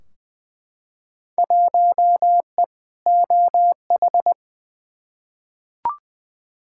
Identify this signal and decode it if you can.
Morse code, heard as a keyed tone, about 21 wpm, 700 Hz. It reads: 1E OH
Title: Field Day Exchanges - 20wpm